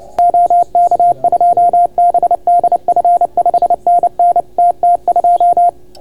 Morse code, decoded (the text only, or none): OK2BDF 5NN TT2